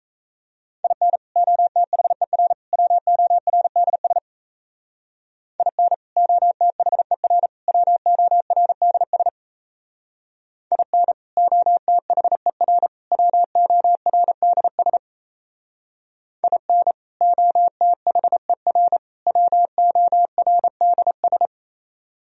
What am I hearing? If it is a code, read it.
Morse code, heard as a keyed tone, about 35 wpm, 700 Hz. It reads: IN OTHER WORDS IN OTHER WORDS IN OTHER WORDS IN OTHER WORDS
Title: Speed-Racing - Sets of 3 Words 28wpm